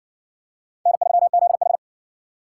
Morse code, heard as a keyed tone, about 56 wpm, 700 Hz. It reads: N3ZH